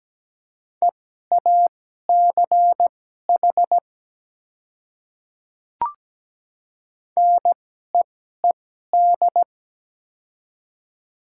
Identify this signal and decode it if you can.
Morse code, heard as a keyed tone, about 17 wpm, spaced as for 12 wpm, 700 Hz. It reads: EACH NEED
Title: 2x Character Spacing - Top 300 Words - 17wpm